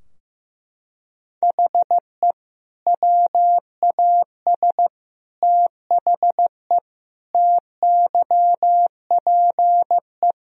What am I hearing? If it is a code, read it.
Morse code, heard as a keyed tone, about 15 wpm, 700 Hz. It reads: HE WAS THE TYPE
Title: Sentences from Top 1000 Words - Review Part 3 - 15wpm